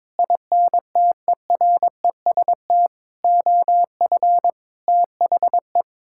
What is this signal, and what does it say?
Morse code, heard as a keyed tone, about 22 wpm, 700 Hz. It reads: INTEREST OF THE
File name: practice / 22wpm / Sets of 6 Words - 22wpm